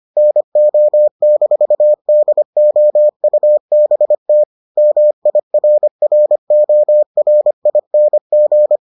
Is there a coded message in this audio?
Yes: Morse code from 0.2 to 8.8 s, heard as a keyed tone, about 25 wpm, 600 Hz. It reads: NO-DOUBT MIRRORING